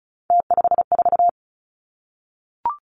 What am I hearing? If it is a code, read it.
Morse code, heard as a keyed tone, about 36 wpm, 700 Hz. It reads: T54
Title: Three Challenging Letter-Number Combinations 35wpm